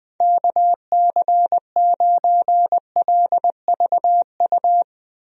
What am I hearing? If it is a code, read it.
Morse code, heard as a keyed tone, about 20 wpm, 700 Hz. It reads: KC9LVU
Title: USA Call Signs - 6 Characters - 20wpm